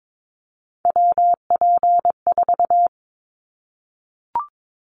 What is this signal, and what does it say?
Morse code, heard as a keyed tone, about 22 wpm, 700 Hz. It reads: WP4
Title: Callsign Prefixes for Top 25 DXCC - 22wpm